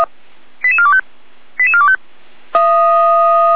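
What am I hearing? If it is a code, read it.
Five-tone selective call (ZVEI), heard as quick runs of five tones, tones of about 70 ms. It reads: 88425 88425
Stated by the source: ZVEI selective call received by SDR, logged by the Feuerwehr Orlamünde (fire brigade) audio server.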